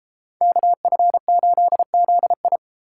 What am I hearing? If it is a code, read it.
Morse code, heard as a keyed tone, about 32 wpm, 700 Hz. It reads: KF8ZI